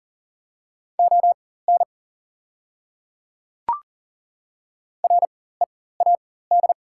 Morse code, heard as a keyed tone, about 40 wpm, spaced as for 19 wpm, 700 Hz. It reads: ON READ
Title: 4x Character Spacing - Top 200 Words - 40wpm